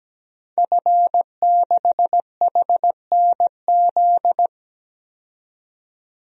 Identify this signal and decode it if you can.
Morse code, heard as a keyed tone, about 17 wpm, 700 Hz. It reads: F6HNZ
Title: DXCC-100 Call Signs - 5 Characters - 17wpm